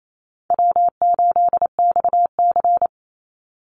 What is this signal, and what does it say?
Morse code, heard as a keyed tone, about 28 wpm, 700 Hz. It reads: W8XC
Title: Easy Call Signs - 28wpm